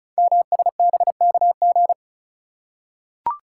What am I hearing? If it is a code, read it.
Morse code, heard as a keyed tone, about 36 wpm, 690 Hz. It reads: MSBKG
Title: Code Groups - 5 Letters - 35wpm